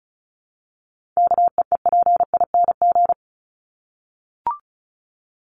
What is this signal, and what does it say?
Morse code, heard as a keyed tone, about 34 wpm, 700 Hz. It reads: KEEPING
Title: ING Suffix - Top 100 Words with Base Words 35wpm